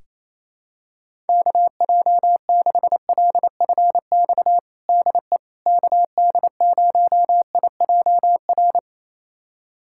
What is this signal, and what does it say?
Morse code, heard as a keyed tone, about 27 wpm, 700 Hz. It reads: KJ6LFX DE KD0IJR
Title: USA Callsign DE Callsign - 6 Characters - 28wpm